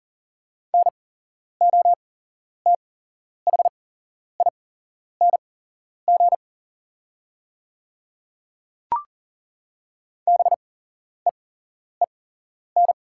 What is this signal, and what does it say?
Morse code, heard as a keyed tone, about 40 wpm, spaced as for 11 wpm, 700 Hz. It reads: NOTHING BEEN
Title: 8x Character Spacing - Top 500 Words - 40wpm